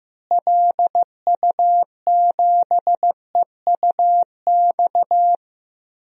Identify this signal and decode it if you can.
Morse code, heard as a keyed tone, about 15 wpm, 700 Hz. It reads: LU7EUX
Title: DXCC-100 Call Signs - 6 Characters - Repeated 3x - 15wpm